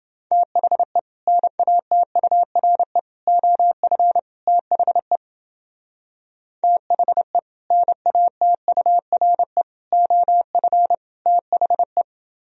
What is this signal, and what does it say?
Morse code, heard as a keyed tone, about 29 wpm, 700 Hz. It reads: THE NATURE OF THE THE NATURE OF THE